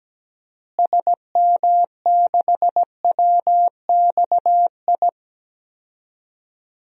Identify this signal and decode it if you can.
Morse code, heard as a keyed tone, about 17 wpm, 700 Hz. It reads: SM6WXI
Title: DXCC-100 Call Signs - 6 Characters - Repeated 2x - 17wpm